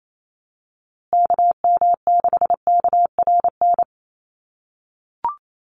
Morse code, heard as a keyed tone, about 28 wpm, 700 Hz. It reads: KM6KRN
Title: USA Call Signs Repeated 2x 28wpm